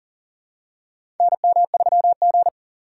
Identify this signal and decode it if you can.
Morse code, heard as a keyed tone, about 40 wpm, 700 Hz. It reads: NM3G
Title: USA Call Signs - 4 Characters - 4x Character Spacing - 40wpm